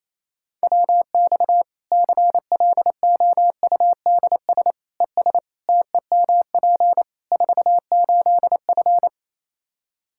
Morse code, heard as a keyed tone, about 27 wpm, 700 Hz. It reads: WX CLOUDS ES TEMP 48F
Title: Practice QSO - Exchange 2 - Info - 28wpm